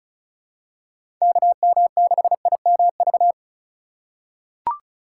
Morse code, heard as a keyed tone, about 34 wpm, 700 Hz. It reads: KM6IMV
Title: USA Call Signs - 6 Characters - 3x Character Spacing - 35wpm